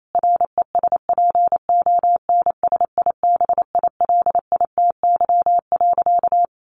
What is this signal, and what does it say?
Morse code, heard as a keyed tone, about 28 wpm, 700 Hz. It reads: RESPONSIBILITY.
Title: Sentences from Top 1200 Words 28wpm